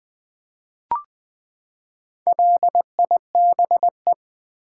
Morse code, heard as a keyed tone, about 20 wpm, 700 Hz. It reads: LIBE
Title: Build A Word - 4 Letters At A Time - 10-Letter Words - 20wpm